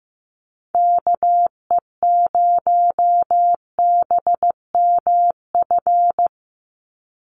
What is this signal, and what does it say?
Morse code, heard as a keyed tone, about 15 wpm, 700 Hz. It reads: KE0BMF
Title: USA Call Signs 15wpm